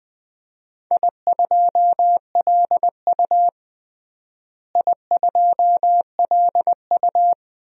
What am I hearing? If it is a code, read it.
Morse code, heard as a keyed tone, about 21 wpm, 700 Hz. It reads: I2LU I2LU
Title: DXCC-100 Call Signs - 4 Characters - Repeated 2x - 20wpm